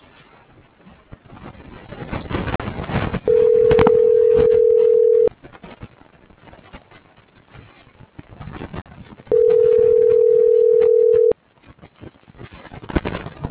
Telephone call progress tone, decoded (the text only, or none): ringback tone